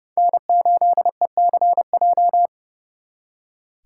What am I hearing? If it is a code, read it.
Morse code, heard as a keyed tone, about 36 wpm, 700 Hz. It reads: N8ECJ
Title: Speed Racing - USA 1x3 Call Signs 30wpm